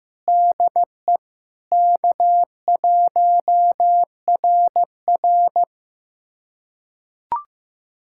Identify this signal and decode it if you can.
Morse code, heard as a keyed tone, about 15 wpm, 700 Hz. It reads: DE K1RR
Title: USA Callsign DE Callsign - 4 Characters - 15wpm